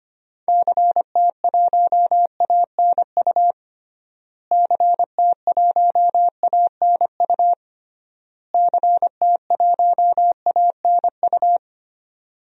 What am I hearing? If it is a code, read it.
Morse code, heard as a keyed tone, about 25 wpm, 700 Hz. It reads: CT1ANU CT1ANU CT1ANU
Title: DXCC-100 Call Signs - 6 Characters - Repeated 3x - 25wpm